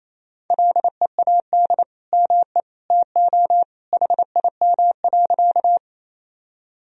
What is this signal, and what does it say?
Morse code, heard as a keyed tone, about 27 wpm, 700 Hz. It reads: LEAD ME TO HIM.
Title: Sentences from Top 1000 Words - Review Part 2 - 28wpm